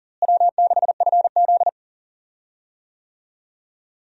Morse code, heard as a keyed tone, about 40 wpm, 700 Hz. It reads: W6FZ
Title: USA Call Signs - 4 Characters - Repeated 2x - 40wpm